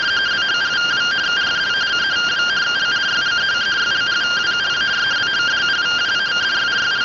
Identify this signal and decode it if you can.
Two-tone teleprinter signal, about 170 Hz shift.